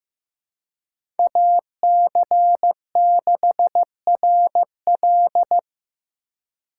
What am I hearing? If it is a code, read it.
Morse code, heard as a keyed tone, about 15 wpm, 700 Hz. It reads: AC6RL
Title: USA Call Signs - 5 Characters - 2x Character Spacing - 15wpm